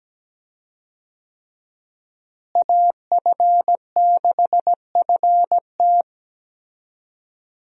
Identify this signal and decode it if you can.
Morse code, heard as a keyed tone, about 17 wpm, 700 Hz. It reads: AF6FT
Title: USA Call Signs - 5 Characters - Repeated 3x - 17wpm